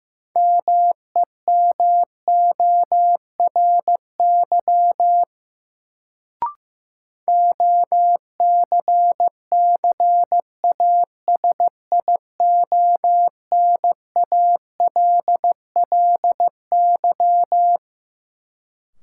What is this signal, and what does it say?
Morse code, heard as a keyed tone, about 15 wpm, 700 Hz. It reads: MEMORY OCCASIONALLY
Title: Sentences from Top 1300 Words - New Word Drill - 15wpm